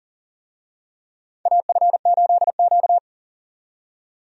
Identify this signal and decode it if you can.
Morse code, heard as a keyed tone, about 40 wpm, 700 Hz. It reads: AF8Q
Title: USA Call Signs - 4 Characters - 40wpm